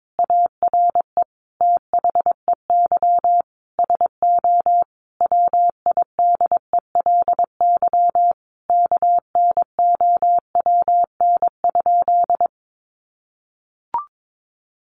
Morse code, heard as a keyed tone, about 22 wpm, 700 Hz. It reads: ARE THEY SO WIDELY KNOWN?
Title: Sentences from Top 1200 Words 22wpm